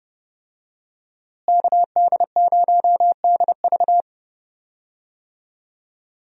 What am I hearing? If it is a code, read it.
Morse code, heard as a keyed tone, about 30 wpm, 700 Hz. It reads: KD0DV
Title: USA Call Signs - 5 Characters - Repeated 3x - 30wpm